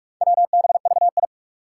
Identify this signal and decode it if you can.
Morse code, heard as a keyed tone, about 44 wpm, 700 Hz. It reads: WBVI
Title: Four Challenging Letter-Number Combinations 45wpm